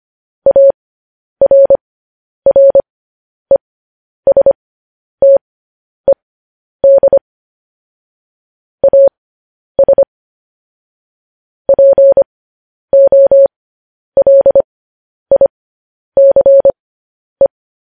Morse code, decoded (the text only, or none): ARRESTED AS POLICE